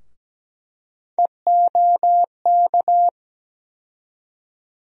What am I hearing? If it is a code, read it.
Morse code, heard as a keyed tone, about 17 wpm, 700 Hz. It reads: EOK
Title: USA Call Sign Postfixes - 3 Letters 17wpm